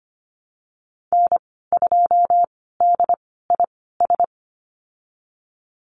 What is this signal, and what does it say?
Morse code, heard as a keyed tone, about 25 wpm, spaced as for 16 wpm, 700 Hz. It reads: N2DIS
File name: Lesson 15g - Number 5 - Call Signs 25wpm